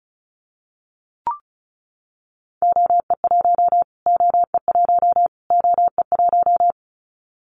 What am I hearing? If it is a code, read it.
Morse code, heard as a keyed tone, about 36 wpm, 700 Hz. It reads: OE1 OE1 OE1
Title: Callsign Prefixes for Top 25 DXCC - Repeated 3x - 35wpm